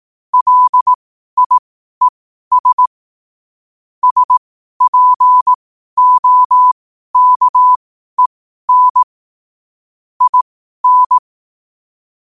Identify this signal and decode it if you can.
Morse code, heard as a keyed tone, about 18 wpm, spaced as for 12 wpm, 1000 Hz. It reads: LIES SPOKEN IN